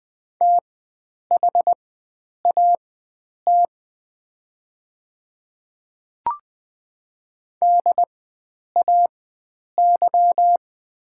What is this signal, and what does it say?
Morse code, heard as a keyed tone, about 21 wpm, spaced as for 9 wpm, 690 Hz. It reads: THAT DAY
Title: 4x Character Spacing - Top 100 Words - 20wpm